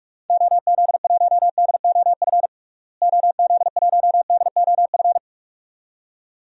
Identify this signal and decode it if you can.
Morse code, heard as a keyed tone, about 46 wpm, 700 Hz. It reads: OZ1DOF OZ1DOF
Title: DXCC-100 Call Signs - 6 Characters - Repeated 2x - 45wpm